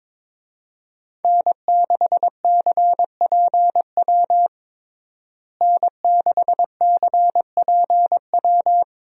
Morse code, heard as a keyed tone, about 22 wpm, 700 Hz. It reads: N6CPW N6CPW
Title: USA Call Signs - 5 Characters - Repeated 2x - 22wpm